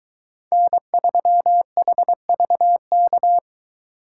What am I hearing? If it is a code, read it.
Morse code, heard as a keyed tone, about 23 wpm, 700 Hz. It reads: N3HVK